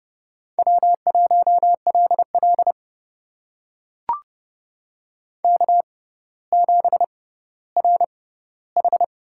Morse code, heard as a keyed tone, about 30 wpm, spaced as for 15 wpm, 700 Hz. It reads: W1LL K7RH
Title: USA Call Signs - 4 Characters - 6x Character Spacing - 30wpm